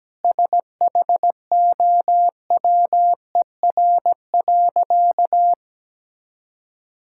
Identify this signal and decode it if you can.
Morse code, heard as a keyed tone, about 17 wpm, 700 Hz. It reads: SHOWER.
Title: Sentences from Top 1600 Words 17wpm